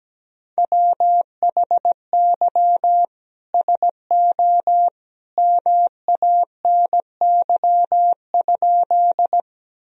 Morse code, heard as a keyed tone, about 17 wpm, 700 Hz. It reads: WHY SO MANY?
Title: Sentences from Top 1000 Words - Review Part 1 - 17wpm